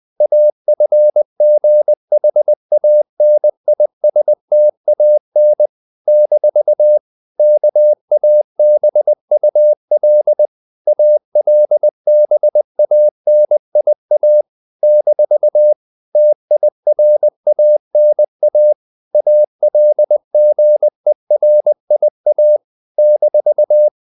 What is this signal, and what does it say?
Morse code, heard as a keyed tone, about 20 wpm, 600 Hz. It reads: AFGHANISTAN - KABUL ALBANIA - TIRANA ALGERIA -
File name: CW / Practice / NationalCapitals20